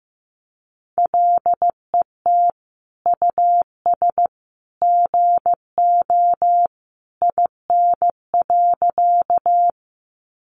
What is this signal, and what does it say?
Morse code, heard as a keyed tone, about 15 wpm, 700 Hz. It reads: LET US GO IN.
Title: Sentences from Top 300 Words 15wpm